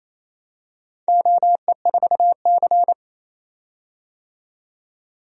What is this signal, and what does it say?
Morse code, heard as a keyed tone, about 27 wpm, 690 Hz. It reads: OE4C
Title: DXCC-100 Call Signs - 4 Characters - 28wpm